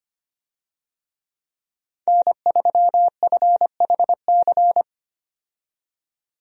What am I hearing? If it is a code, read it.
Morse code, heard as a keyed tone, about 25 wpm, 700 Hz. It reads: N3FHC
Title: USA Call Signs - 5 Characters - 25wpm